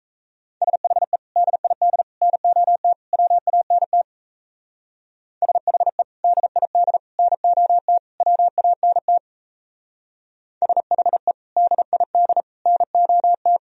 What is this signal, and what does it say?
Morse code, heard as a keyed tone, about 35 wpm, 700 Hz. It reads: SHE DID NOT WANT SHE DID NOT WANT SHE DID NOT
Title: Speed-Racing - Sets of 4 Words 28wpm